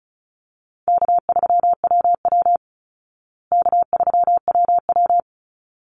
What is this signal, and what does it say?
Morse code, heard as a keyed tone, about 34 wpm, 700 Hz. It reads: K3WW K3WW
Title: Easy Call Signs Repeated 2x - 35wpm